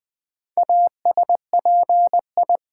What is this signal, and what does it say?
Morse code, heard as a keyed tone, about 20 wpm, 700 Hz. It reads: ASPI
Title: Build A Word - 2 Letters At A Time - 6-Letter Words - 20wpm